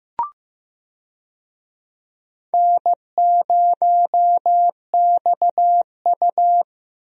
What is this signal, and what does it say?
Morse code, heard as a keyed tone, about 15 wpm, 700 Hz. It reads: N0XU
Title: USA Call Signs - 4 Characters - Repeated 2x - 15wpm